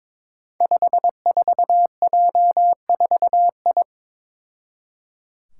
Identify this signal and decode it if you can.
Morse code, heard as a keyed tone, about 22 wpm, 700 Hz. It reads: 54J4I
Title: Five Challenging Letter-Number Combinations 22wpm